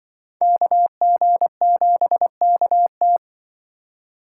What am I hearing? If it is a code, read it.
Morse code, heard as a keyed tone, about 26 wpm, 700 Hz. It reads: KG7KT